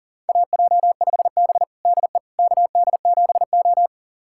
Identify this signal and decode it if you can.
Morse code, heard as a keyed tone, about 40 wpm, 700 Hz. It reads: AJ5B DE KD7O
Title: USA Callsign DE Callsign - 4 Characters - 40wpm